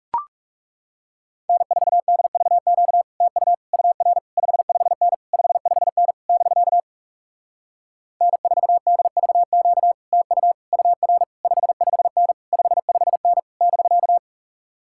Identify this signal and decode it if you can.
Morse code, heard as a keyed tone, about 40 wpm, 700 Hz. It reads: N4DVQ TU UR 55N 55N <BK> N4DVQ TU UR 55N 55N <BK>